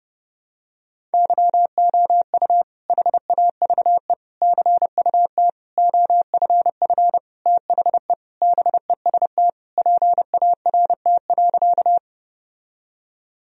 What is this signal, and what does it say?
Morse code, heard as a keyed tone, about 31 wpm, 700 Hz. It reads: YOU HAVE CUT OFF THE BEST PART.
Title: Sentences from Top 1000 Words - Review Part 1 - 30wpm